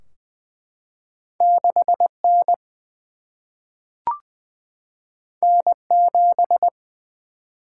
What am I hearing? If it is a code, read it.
Morse code, heard as a keyed tone, about 20 wpm, 700 Hz. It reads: 6 N N 7
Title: Single-Letter Single-Number 20wpm